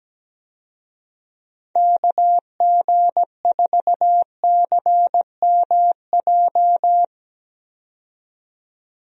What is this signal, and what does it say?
Morse code, heard as a keyed tone, about 17 wpm, 700 Hz. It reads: KG4CMJ
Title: USA Call Signs - 6 Characters - 17wpm